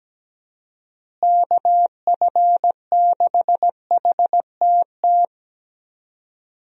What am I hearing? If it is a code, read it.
Morse code, heard as a keyed tone, about 17 wpm, 700 Hz. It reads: KF6HTT
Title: USA Call Signs - 6 Characters - Repeated 3x - 17wpm